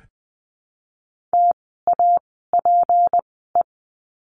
Morse code, heard as a keyed tone, about 20 wpm, spaced as for 14 wpm, 700 Hz. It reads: TAPE
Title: 153 Lesson 25f - Letter p - Words 20wpm